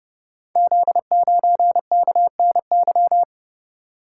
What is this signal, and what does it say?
Morse code, heard as a keyed tone, about 30 wpm, 700 Hz. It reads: Z9KNY
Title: Code Groups - 5 Letter:Numbers - 30wpm